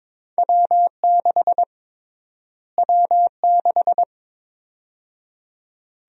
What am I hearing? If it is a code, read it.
Morse code, heard as a keyed tone, about 22 wpm, 700 Hz. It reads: W6 W6
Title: USA Call Sign Prefixes - Repeated 2x - 22wpm